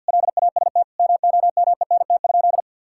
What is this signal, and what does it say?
Morse code, heard as a keyed tone, about 50 wpm, 700 Hz. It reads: LAST MOMENT?
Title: Sentences from Top 1600 Words 50wpm